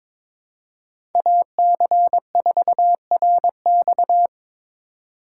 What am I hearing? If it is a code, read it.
Morse code, heard as a keyed tone, about 22 wpm, 700 Hz. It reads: AC4RX
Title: USA Call Signs - 5 Characters - 4x Character Spacing - 22wpm